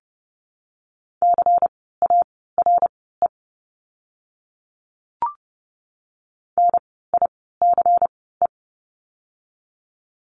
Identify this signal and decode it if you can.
Morse code, heard as a keyed tone, about 30 wpm, spaced as for 17 wpm, 700 Hz. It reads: CARE NICE
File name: Lesson 16f - Letter c - Words 30wpm